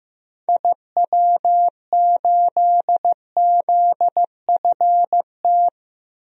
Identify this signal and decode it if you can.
Morse code, heard as a keyed tone, about 15 wpm, 700 Hz. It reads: IW8ZFT
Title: DXCC-100 Call Signs - 6 Characters - 15wpm